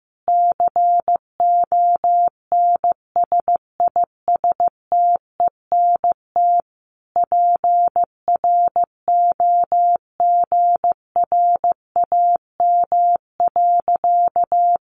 Morse code, heard as a keyed tone, about 15 wpm, 700 Hz. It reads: CONSISTENT PROGRAM.
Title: Sentences from Top 1500 Words 15wpm